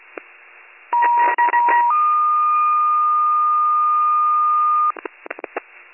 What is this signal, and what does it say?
Two-tone page, heard as a one-second tone, then a three-second tone, about 950 then 1220 Hz, with noise. It is heard from 0.9 to 4.9 s.